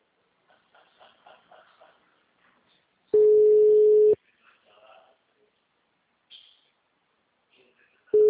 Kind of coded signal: telephone call progress tone

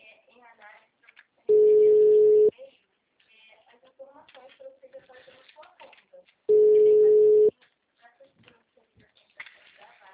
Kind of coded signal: telephone call progress tone

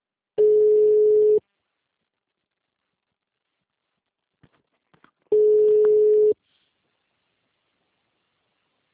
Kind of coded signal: telephone call progress tone

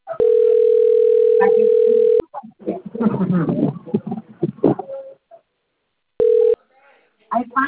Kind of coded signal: telephone call progress tone

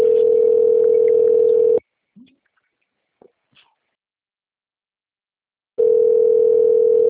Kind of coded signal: telephone call progress tone

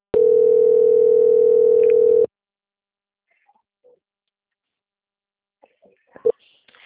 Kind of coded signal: telephone call progress tone

